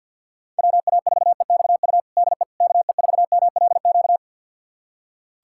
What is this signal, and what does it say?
Morse code, heard as a keyed tone, about 47 wpm, 700 Hz. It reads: WA4EXU DE KE4MLQ
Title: USA Callsign DE Callsign - 6 Characters - 50wpm